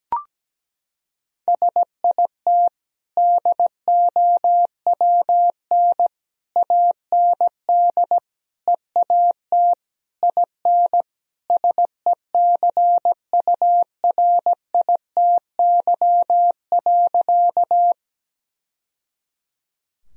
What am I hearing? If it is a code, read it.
Morse code, heard as a keyed tone, about 17 wpm, 700 Hz. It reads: SIT DOWN AND EAT IN SECURITY.
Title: Sentences from Top 1000 Words - Review Part 4 - 17wpm